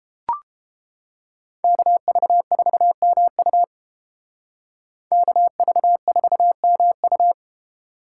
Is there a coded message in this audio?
Yes: Morse code from 1.6 to 7.3 s, heard as a keyed tone, about 31 wpm, 700 Hz. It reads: KV4MU KV4MU